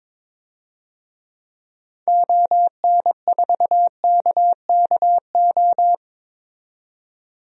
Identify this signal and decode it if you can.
Morse code, heard as a keyed tone, about 22 wpm, 700 Hz. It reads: ON4KKO
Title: DXCC-100 Call Signs - 6 Characters - 22wpm